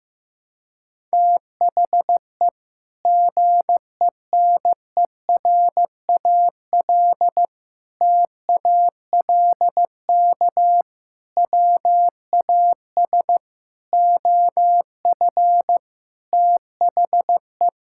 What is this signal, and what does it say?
Morse code, heard as a keyed tone, about 15 wpm, 700 Hz. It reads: THE GENERAL TALK WAS OF THE
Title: Sentences from Top 1000 Words - Review Part 5 - 15wpm